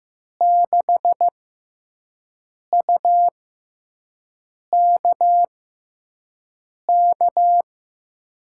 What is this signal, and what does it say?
Morse code, heard as a keyed tone, about 15 wpm, spaced as for 5 wpm, 700 Hz. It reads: 6UKK
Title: USA Call Signs - 5 Characters - 6x Character Spacing - 15wpm